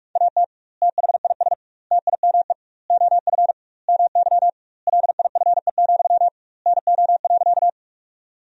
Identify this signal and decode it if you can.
Morse code, heard as a keyed tone, about 46 wpm, 700 Hz. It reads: AT THIS TIME OF MY LIFE, NO.
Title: 2x Word-Spacing - Sentences from Top 300 Words - 45wpm